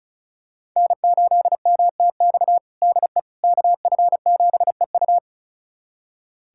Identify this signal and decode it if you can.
Morse code, heard as a keyed tone, about 36 wpm, 700 Hz. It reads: N8MTX DE KF7EU
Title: USA Callsign DE Callsign - 5 Characters - 35wpm